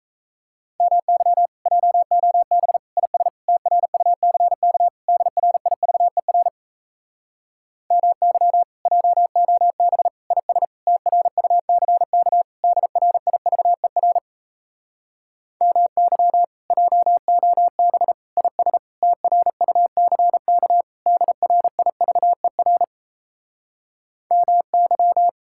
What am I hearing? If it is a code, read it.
Morse code, heard as a keyed tone, about 38 wpm, 700 Hz. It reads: MY JOB IS TRUCK DRIVER MY JOB IS TRUCK DRIVER MY JOB IS TRUCK DRIVER MY